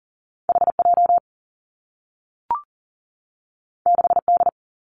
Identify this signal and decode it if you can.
Morse code, heard as a keyed tone, about 40 wpm, 700 Hz. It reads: H J 6 D
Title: Two Challenging Letter-Number Combinations 40wpm